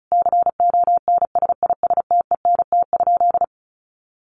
Morse code, heard as a keyed tone, about 34 wpm, 700 Hz. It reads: CONSISTENT?
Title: Sentences from Top 1500 Words 35wpm